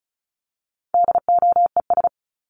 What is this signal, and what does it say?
Morse code, heard as a keyed tone, about 34 wpm, 700 Hz. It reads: DOES
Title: ES:S Suffix - Top 100 Words with Base Words 35wpm